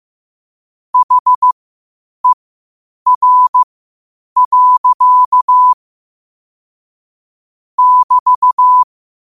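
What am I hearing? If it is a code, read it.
Morse code, heard as a keyed tone, about 15 wpm, spaced as for 8 wpm, 1000 Hz. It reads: HER. <BT>
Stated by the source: Morse code quotes at 8 WPM.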